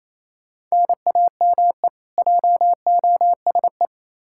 Morse code, heard as a keyed tone, about 28 wpm, 700 Hz. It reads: NAME JOSE
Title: Practice QSO - Exchange 1 - Beginning - 28wpm